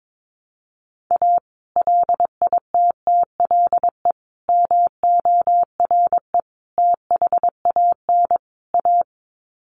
Morse code, heard as a keyed tone, about 22 wpm, 700 Hz. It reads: A LITTLE MORE THAN A
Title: Sets of 5 Words Encore2 22wpm